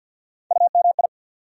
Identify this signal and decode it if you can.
Morse code, heard as a keyed tone, about 53 wpm, 700 Hz. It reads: UMI
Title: USA Call Sign Postfixes - 3 Letters 50wpm